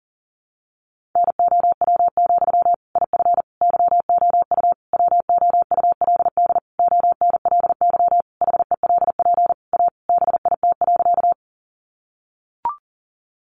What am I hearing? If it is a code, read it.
Morse code, heard as a keyed tone, about 40 wpm, 700 Hz. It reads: NOW, IF YOU WOULD ONLY HELP A BIT.
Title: Sentences from Top 900 Words 40wpm